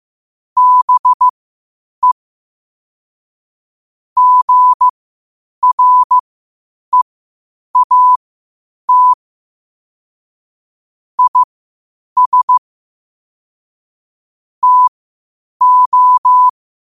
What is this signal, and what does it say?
Morse code, heard as a keyed tone, about 15 wpm, spaced as for 8 wpm, 1000 Hz. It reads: BE GREAT IS TO
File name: qotdfiles / 8wpm